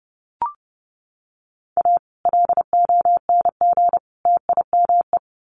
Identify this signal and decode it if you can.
Morse code, heard as a keyed tone, about 30 wpm, 700 Hz. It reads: A LONG TIME
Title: Sets of 3 Words 30wpm